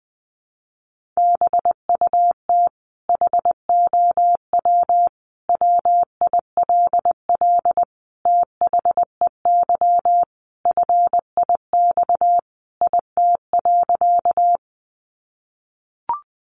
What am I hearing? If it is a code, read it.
Morse code, heard as a keyed tone, about 20 wpm, 700 Hz. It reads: BUT HOW WILL THEY FIX IT.
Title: Sentences from Top 1400 Words 20wpm